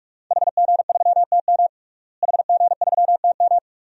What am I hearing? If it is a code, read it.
Morse code, heard as a keyed tone, about 46 wpm, 700 Hz. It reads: HG3TM HG3TM
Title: DXCC-100 Call Signs - 5 Characters - Repeated 2x - 45wpm